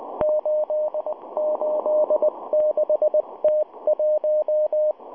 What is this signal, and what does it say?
Morse code, heard as a keyed tone, about 19 wpm, 610 Hz. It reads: 886T1